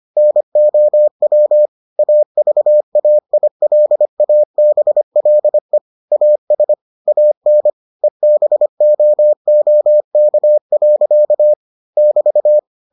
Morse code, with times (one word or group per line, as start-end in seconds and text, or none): 0.2-1.7 NOW
2.0-5.8 AVAILABLE
6.1-6.7 AS
7.1-7.7 AN
8.0-11.5 EBOOK.
12.0-12.6 <BT>